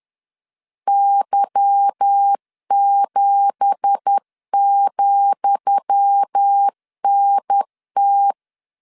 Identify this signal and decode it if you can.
Morse code, heard as a keyed tone, about 10 wpm, 800 Hz. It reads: Y7,NT
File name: LEZIONI / 10 SIGNS / 10D COMA 10wpm 3013